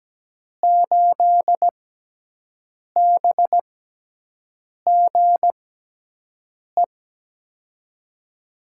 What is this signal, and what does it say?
Morse code, heard as a keyed tone, about 17 wpm, spaced as for 6 wpm, 700 Hz. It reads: 8BGE
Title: USA Call Signs - 6 Characters - 6x Character Spacing - 17wpm